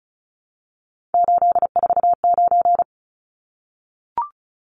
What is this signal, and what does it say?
Morse code, heard as a keyed tone, about 34 wpm, 700 Hz. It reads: 849
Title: Numbers - 3-Digit 35wpm